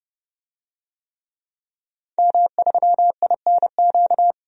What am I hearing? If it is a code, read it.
Morse code, heard as a keyed tone, about 32 wpm, 700 Hz. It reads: M3INQ
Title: DXCC-100 Call Signs - 5 Characters - 30wpm